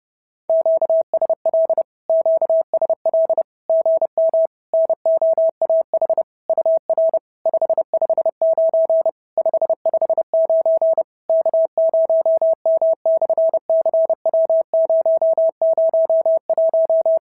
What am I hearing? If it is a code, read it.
Morse code, heard as a keyed tone, about 30 wpm, 650 Hz. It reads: QSL QSL GM NOAH UR 559 559 K0M/CW001
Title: SOTA QSO Practice 30wpm